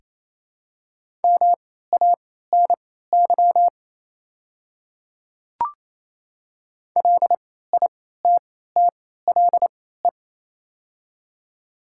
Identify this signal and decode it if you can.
Morse code, heard as a keyed tone, about 28 wpm, spaced as for 16 wpm, 700 Hz. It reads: MANY LITTLE
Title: 3x Character Spacing - Top 200 Words - 28wpm